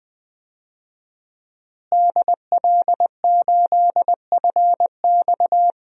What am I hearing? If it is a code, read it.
Morse code, heard as a keyed tone, about 20 wpm, 700 Hz. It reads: DL8FX
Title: DXCC-100 Call Signs - 5 Characters - Repeated 3x - 20wpm